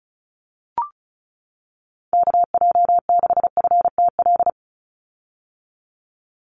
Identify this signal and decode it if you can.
Morse code, heard as a keyed tone, about 34 wpm, 700 Hz. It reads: KJ6FTL
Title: USA Call Signs 35wpm